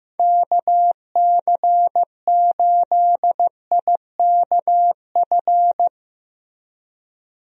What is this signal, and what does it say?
Morse code, heard as a keyed tone, about 15 wpm, 700 Hz. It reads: KC8IKF
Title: USA Call Signs - 6 Characters - Repeated 3x - 15wpm